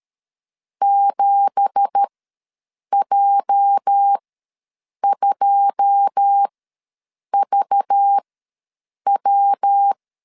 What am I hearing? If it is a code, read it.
Morse code, heard as a keyed tone, about 13 wpm, spaced as for 7 wpm, 800 Hz. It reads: 7J2VW
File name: LEZIONI / 10 SIGNS / 10C DOT ws 12wpm 3548